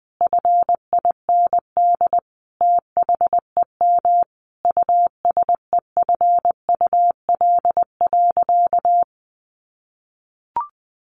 Morse code, heard as a keyed tone, about 20 wpm, 700 Hz. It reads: FIND THEM USEFUL.
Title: Sentences from Top 700 Words 20wpm